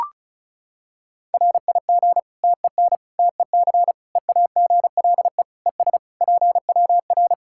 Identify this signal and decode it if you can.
Morse code, heard as a keyed tone, about 36 wpm, 700 Hz. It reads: RIG TEN TEC EAGLE ES PWR
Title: Practice QSO - Exchange 2 - Info - 35wpm